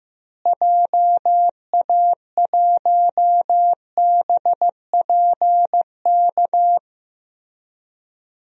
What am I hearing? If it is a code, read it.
Morse code, heard as a keyed tone, about 15 wpm, 700 Hz. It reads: JA1BPK
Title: DXCC-100 Call Signs - 6 Characters - Repeated 3x - 15wpm